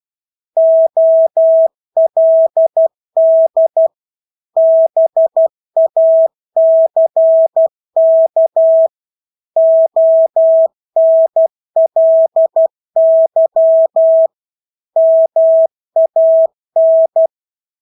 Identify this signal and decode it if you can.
Morse code, heard as a keyed tone, about 12 wpm, 650 Hz. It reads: OLD BACK ONLY MAN
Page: Engelske ord 12 wpm